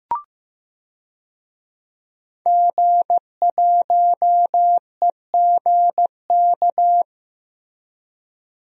Morse code, heard as a keyed tone, about 15 wpm, 700 Hz. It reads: G1EGK
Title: DXCC-100 Call Signs - 5 Characters - 15wpm